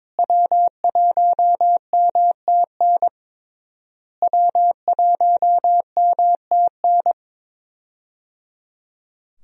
Morse code, heard as a keyed tone, about 22 wpm, 700 Hz. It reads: W1MTN W1MTN
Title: USA Call Signs - 5 Characters - Repeated 3x - 22wpm